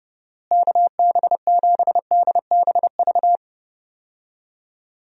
Morse code, heard as a keyed tone, about 30 wpm, 700 Hz. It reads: KB7DBV
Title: USA Call Signs - 6 Characters - Repeated 2x - 30wpm